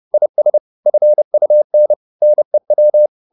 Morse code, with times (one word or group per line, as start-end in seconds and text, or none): 0.1-0.6 IS
0.9-1.9 FUN
2.2-3.1 NEW